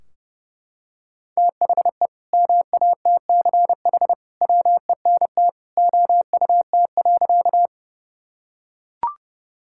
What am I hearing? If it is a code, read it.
Morse code, heard as a keyed tone, about 30 wpm, 700 Hz. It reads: THE MATCH WENT OUT.
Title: Sentences from Top 1600 Words 30wpm